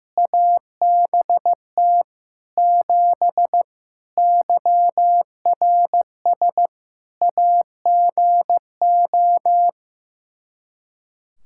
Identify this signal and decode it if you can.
Morse code, heard as a keyed tone, about 15 wpm, 700 Hz. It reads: ABT 7 YRS AGO
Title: K1USN SST QSO Practice 15wpm